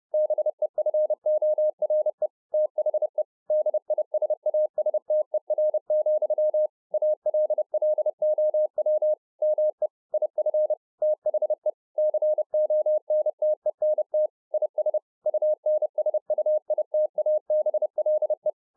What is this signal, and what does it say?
Morse code, heard as a keyed tone, about 30 wpm, 610 Hz. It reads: BEFORE THE DISASTER, ALLOW ME IF THE CONTENT IS UNSUITABLE